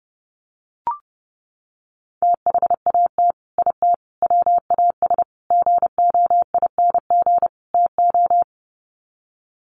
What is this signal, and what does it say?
Morse code, heard as a keyed tone, about 30 wpm, 700 Hz. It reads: THAT IT WAS GOING TO
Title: Sets of 5 Words Encore2 30wpm